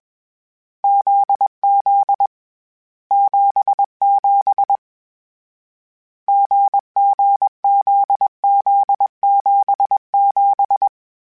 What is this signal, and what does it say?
Morse code, heard as a keyed tone, about 21 wpm, 800 Hz. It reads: ZZ 77 GGZZ77
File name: Difficult Combos / Difficult Combos-mp3s / GZ GZ7 @20